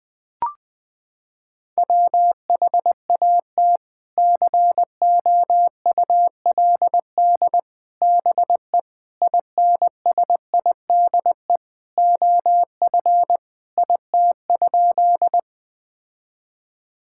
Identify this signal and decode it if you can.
Morse code, heard as a keyed tone, about 21 wpm, 700 Hz. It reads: WHAT COULD BE INSIDE OF IT?
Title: Sentences from Top 1000 Words - Review Part 6 - 20wpm